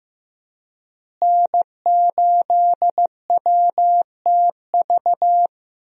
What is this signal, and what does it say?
Morse code, heard as a keyed tone, about 15 wpm, 700 Hz. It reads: N8WTV
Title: USA Call Signs - 5 Characters - Repeated 2x - 15wpm